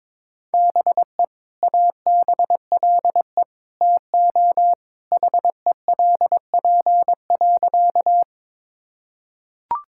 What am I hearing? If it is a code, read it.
Morse code, heard as a keyed tone, about 22 wpm, 700 Hz. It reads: BE ABLE TO HELP.
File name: practice / 22wpm / Sentences with 2-Word Phrases - 22wpm